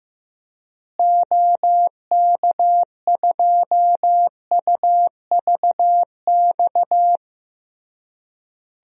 Morse code, heard as a keyed tone, about 15 wpm, 700 Hz. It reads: OK2UVX
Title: DXCC-100 Call Signs - 6 Characters - Repeated 3x - 15wpm